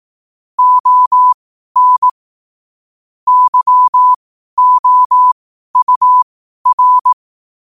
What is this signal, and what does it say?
Morse code, heard as a keyed tone, about 18 wpm, spaced as for 12 wpm, 1000 Hz. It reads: ON YOUR
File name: qotdfiles / 12wpm